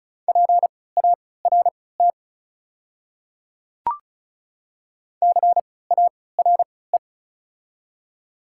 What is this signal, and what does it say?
Morse code, heard as a keyed tone, about 34 wpm, spaced as for 20 wpm, 690 Hz. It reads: PART CARE
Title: 3x Character Spacing - Top 500 Words - 35wpm